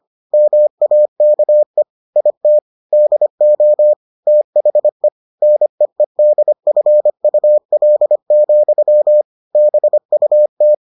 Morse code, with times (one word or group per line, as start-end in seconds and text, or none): 0.3-1.8 MAKE
2.2-2.6 IT
2.9-3.9 DO
4.3-5.1 THE
5.4-9.2 NEEDFUL,
9.6-10.7 BUT